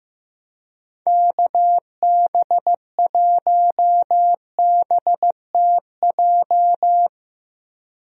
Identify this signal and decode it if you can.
Morse code, heard as a keyed tone, about 15 wpm, 700 Hz. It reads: KB1BTJ
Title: USA Call Signs - 6 Characters - 15wpm